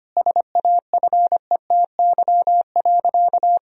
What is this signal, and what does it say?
Morse code, heard as a keyed tone, about 25 wpm, 700 Hz. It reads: SAFETY.
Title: Sentences from Top 1000 Words - Review Part 4 - 25wpm